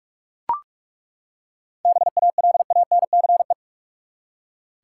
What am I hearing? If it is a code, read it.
Morse code, heard as a keyed tone, about 42 wpm, 700 Hz. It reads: BALANCE
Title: Seven Letter Words - 45wpm